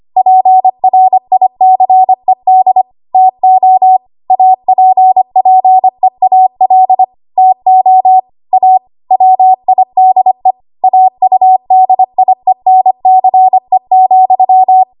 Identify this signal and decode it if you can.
Morse code, heard as a keyed tone, about 25 wpm, 750 Hz. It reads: PRICED TO APPEAL TO A WIDE AUDIENCE,
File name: Archive / 25 WPM / 25WPM